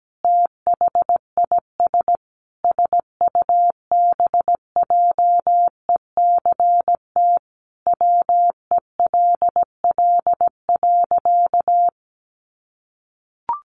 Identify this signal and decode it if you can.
Morse code, heard as a keyed tone, about 17 wpm, 700 Hz. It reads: THIS SUBJECT WELL.
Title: Sentences from Top 800 Words 17wpm